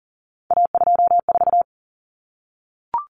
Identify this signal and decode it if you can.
Morse code, heard as a keyed tone, about 40 wpm, 700 Hz. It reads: A24
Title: Common Cut Numbers - 3-Digit 40wpm